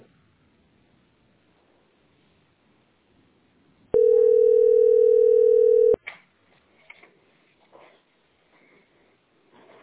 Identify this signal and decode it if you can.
Ringback tone (North America), 440 + 480 Hz.